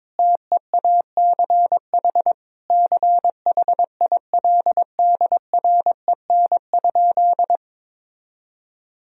Morse code, heard as a keyed tone, about 22 wpm, 700 Hz. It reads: TEACH CHILDREN?
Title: Sentences from Top 1000 Words - Review Part 2 - 22wpm